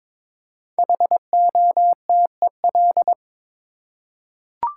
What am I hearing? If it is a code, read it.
Morse code, heard as a keyed tone, about 22 wpm, 700 Hz. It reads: HOTEL
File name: 5 Letter Words - First 3 Letters Given - 22wpm